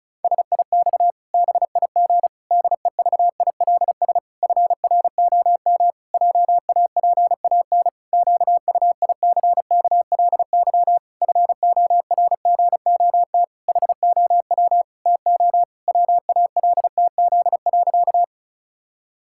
Morse code, heard as a keyed tone, about 36 wpm, 700 Hz. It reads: SIX BIG DEVILS FROM JAPAN QUICKLY FORGOT HOW TO WALTZ.
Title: Pangram Challenge v2 - 35wpm